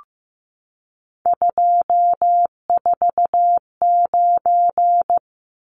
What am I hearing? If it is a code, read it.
Morse code, heard as a keyed tone, about 15 wpm, 700 Hz. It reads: 249
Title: Numbers - 3-Digit 15wpm